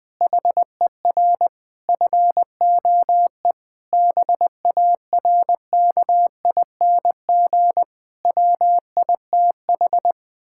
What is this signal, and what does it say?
Morse code, heard as a keyed tone, about 20 wpm, 700 Hz. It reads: HER FOE BARKING WITH
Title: Pangram Challenge v2 - 20wpm